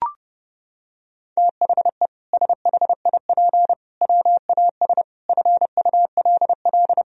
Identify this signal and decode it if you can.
Morse code, heard as a keyed tone, about 30 wpm, 700 Hz. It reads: THE SHIP WAS FULL
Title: Sentences from Top 1000 Words - Review Part 3 - 30wpm